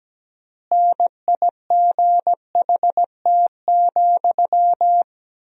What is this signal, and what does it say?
Morse code, heard as a keyed tone, about 17 wpm, 700 Hz. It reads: NIGHT,
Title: 4x Word-Spacing - Sentences from Top 300 Words - 17wpm